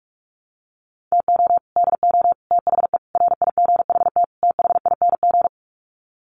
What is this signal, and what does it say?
Morse code, heard as a keyed tone, about 44 wpm, 700 Hz. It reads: TO DO THE RIGHT THING
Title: Sets of 5 Words 45wpm